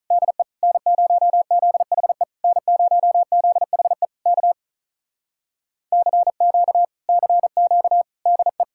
Morse code, heard as a keyed tone, about 46 wpm, 700 Hz. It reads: DE N0ZHE N0ZHE K CQ CQ DE